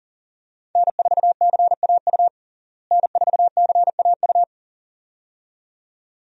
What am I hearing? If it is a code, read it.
Morse code, heard as a keyed tone, about 40 wpm, 700 Hz. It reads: N4CAU N4CAU
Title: USA Call Signs - 5 Characters - Repeated 2x - 40wpm